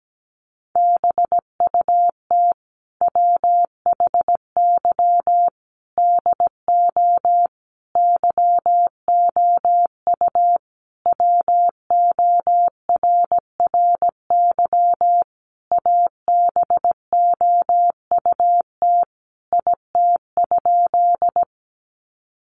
Morse code, heard as a keyed tone, about 17 wpm, 700 Hz. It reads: BUT WHY DO YOU WORRY ABOUT IT?
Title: Sentences from Top 1100 Words 17wpm